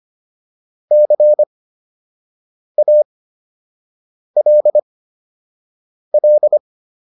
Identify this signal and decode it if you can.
Morse code, heard as a keyed tone, about 25 wpm, spaced as for 6 wpm, 600 Hz. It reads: CALL